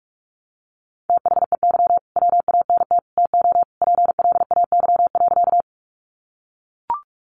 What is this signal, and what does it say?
Morse code, heard as a keyed tone, about 44 wpm, 700 Hz. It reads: THEY WANT TO PLAY.
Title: Easy Phrases 45wpm